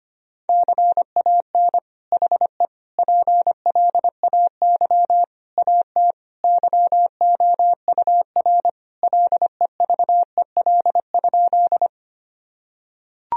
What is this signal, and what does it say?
Morse code, heard as a keyed tone, about 25 wpm, 700 Hz. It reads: CAN HE PLAY AT YOUR LEVEL?
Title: Sentences from Top 1000 Words - Review Part 4 - 25wpm